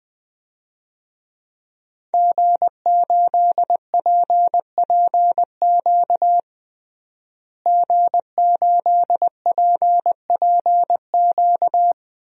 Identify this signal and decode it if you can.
Morse code, heard as a keyed tone, about 21 wpm, 700 Hz. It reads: G8PPQ G8PPQ
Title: DXCC-100 Call Signs - 5 Characters - Repeated 3x - 20wpm